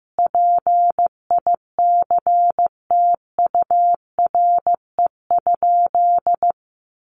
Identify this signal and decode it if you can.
Morse code, heard as a keyed tone, about 15 wpm, 700 Hz. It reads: PICTURE?
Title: A Sound Of Thunder 15wpm